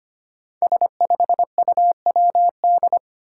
Morse code, heard as a keyed tone, about 25 wpm, 700 Hz. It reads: S5UWD
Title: Five Challenging Letter-Number Combinations 25wpm